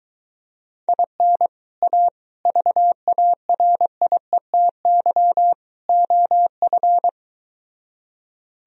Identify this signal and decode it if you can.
Morse code, heard as a keyed tone, about 23 wpm, 700 Hz. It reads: IN A VARIETY OF